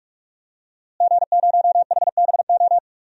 Morse code, heard as a keyed tone, about 46 wpm, 700 Hz. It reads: G0HBO
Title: DXCC-100 Call Signs - 5 Characters - Repeated 3x - 45wpm